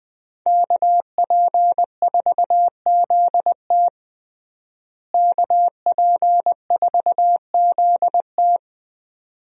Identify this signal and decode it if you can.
Morse code, heard as a keyed tone, about 20 wpm, 700 Hz. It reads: KP4ZT KP4ZT
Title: USA Call Signs - 5 Characters - Repeated 2x - 20wpm